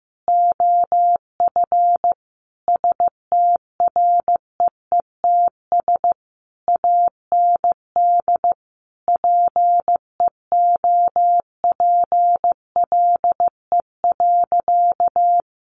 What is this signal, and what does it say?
Morse code, heard as a keyed tone, about 15 wpm, 700 Hz. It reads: OF STREETS AND PEOPLE.
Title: A Sound Of Thunder 15wpm